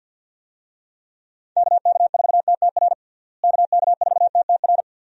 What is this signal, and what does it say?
Morse code, heard as a keyed tone, about 50 wpm, 700 Hz. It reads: KK4TTR KK4TTR
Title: USA Call Signs - 6 Characters - Repeated 2x - 50wpm